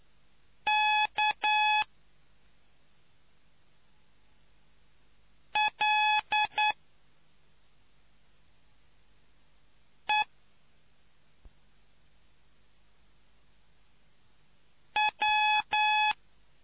Morse code, heard as a keyed tone, about 9 wpm, 850 Hz. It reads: K L E W